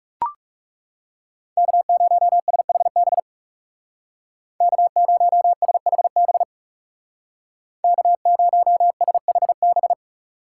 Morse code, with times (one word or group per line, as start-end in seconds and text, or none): 1.6-3.2 K0SHB
4.6-6.4 K0SHB
7.8-9.9 K0SHB